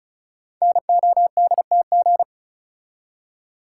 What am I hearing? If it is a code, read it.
Morse code, heard as a keyed tone, about 36 wpm, 700 Hz. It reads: NODTG
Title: Code Groups - 5 Letters - 35wpm